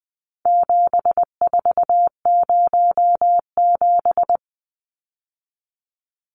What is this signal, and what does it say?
Morse code, heard as a keyed tone, about 20 wpm, 700 Hz. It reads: 7407
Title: Numbers - 4-Digit 20wpm